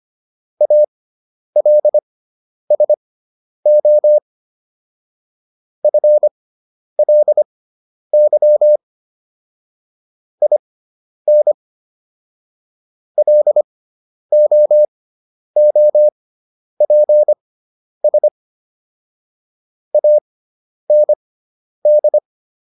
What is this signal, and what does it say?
Morse code, heard as a keyed tone, about 25 wpm, spaced as for 10 wpm, 600 Hz. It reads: ALSO FLY IN LOOPS AND